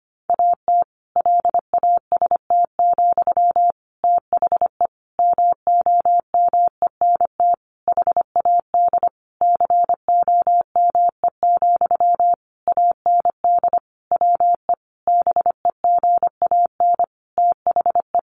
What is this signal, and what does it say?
Morse code, heard as a keyed tone, about 25 wpm, 700 Hz. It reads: AT LAST, THE MOMENT HAD COME, AND WE BEGAN THE
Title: Sentences from Top 1500 Words 25wpm